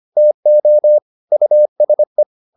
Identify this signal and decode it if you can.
Morse code, heard as a keyed tone, about 25 wpm, 600 Hz. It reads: TO USE